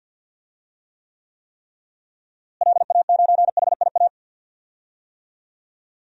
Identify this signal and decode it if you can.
Morse code, heard as a keyed tone, about 52 wpm, 700 Hz. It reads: LA9HIA
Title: DXCC-100 Call Signs - 6 Characters - Repeated 3x - 50wpm